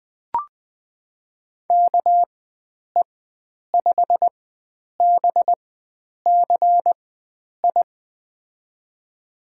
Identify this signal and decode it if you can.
Morse code, heard as a keyed tone, about 20 wpm, spaced as for 9 wpm, 700 Hz. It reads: KE5BCI
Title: USA Call Signs - 6 Characters - 4x Character Spacing - 20wpm